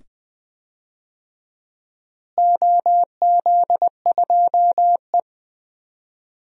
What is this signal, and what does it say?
Morse code, heard as a keyed tone, about 20 wpm, 700 Hz. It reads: OZ2E
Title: DXCC-100 Call Signs - 4 Characters - Repeated 2x - 20wpm